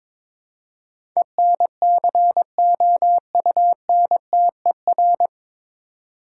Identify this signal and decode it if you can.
Morse code, heard as a keyed tone, about 22 wpm, 700 Hz. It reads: ENCOUNTER
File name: 9 Letter Words - First 3 Letters Given - 22wpm